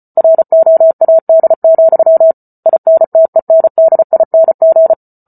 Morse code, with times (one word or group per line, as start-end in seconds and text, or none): 0.2-2.3 ROAD,
2.7-4.9 INTENDING